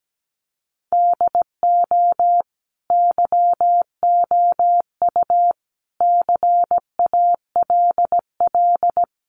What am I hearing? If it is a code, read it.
Morse code, heard as a keyed tone, about 17 wpm, 700 Hz. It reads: DO YOU CALL
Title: Sentences from Top 1500 Words 17wpm